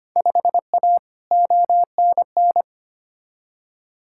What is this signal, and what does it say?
Morse code, heard as a keyed tone, about 25 wpm, 700 Hz. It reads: 5A ONN
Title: Field Day Exchanges Encore - 25wpm